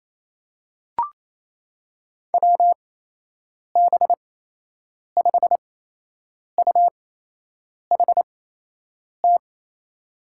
Morse code, heard as a keyed tone, about 28 wpm, spaced as for 8 wpm, 700 Hz. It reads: WB5UHT
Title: USA Call Signs - 6 Characters - 8x Character Spacing - 28wpm